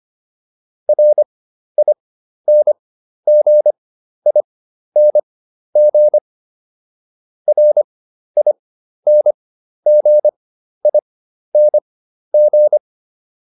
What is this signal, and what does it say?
Morse code, heard as a keyed tone, about 25 wpm, spaced as for 12 wpm, 600 Hz. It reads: RINGING RINGING